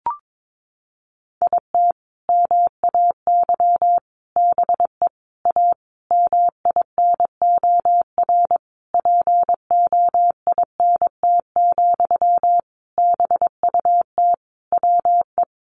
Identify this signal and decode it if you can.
Morse code, heard as a keyed tone, about 22 wpm, 700 Hz. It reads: IT MAY BE A MINOR POINT, BUT WE
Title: Sentences from Top 1500 Words 22wpm